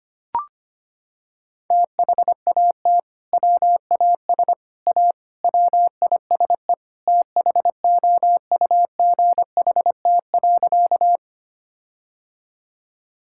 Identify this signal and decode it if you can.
Morse code, heard as a keyed tone, about 25 wpm, 700 Hz. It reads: THAT WAS A WISE THOUGHT.
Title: Sentences from Top 1600 Words 25wpm